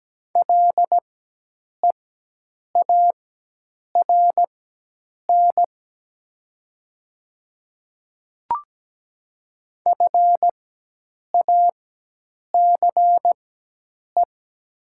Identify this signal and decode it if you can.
Morse code, heard as a keyed tone, about 17 wpm, spaced as for 8 wpm, 700 Hz. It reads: LEARN FACE
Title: 4x Character Spacing - Top 300 Words - 17wpm